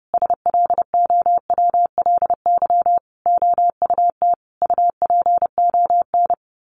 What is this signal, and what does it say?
Morse code, heard as a keyed tone, about 30 wpm, 700 Hz. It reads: SLOWLY OUT UPON
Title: A Sound Of Thunder 30wpm